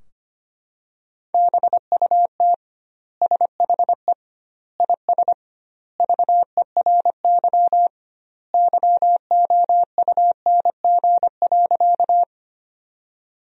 Morse code, heard as a keyed tone, about 25 wpm, 700 Hz. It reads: BUT SHE IS VERY YOUNG.
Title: 2x Word-Spacing - Sentences from Top 400 Words - 25wpm